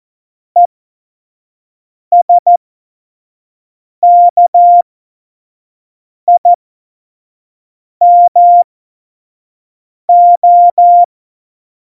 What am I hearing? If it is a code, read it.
Morse code, heard as a keyed tone, about 14 wpm, spaced as for 5 wpm, 700 Hz. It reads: ESKIMO
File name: Morsekurs / Witze / Witz 28. Januar 2020, 25 F70